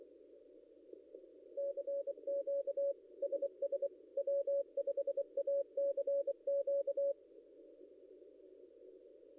Morse code, heard as a keyed tone, about 24 wpm, 560 Hz. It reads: CQ SS W5ACQ
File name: contests / ss-cw-k5tr-op-n5rz -0800